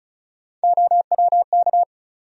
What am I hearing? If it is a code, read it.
Morse code, heard as a keyed tone, about 34 wpm, 700 Hz. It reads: OWK
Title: USA Call Sign Postfixes - 3 Letters 35wpm